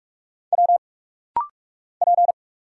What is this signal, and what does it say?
Morse code, heard as a keyed tone, about 46 wpm, 700 Hz. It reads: W P
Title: Single Letters - Rapid-Fire 45wpm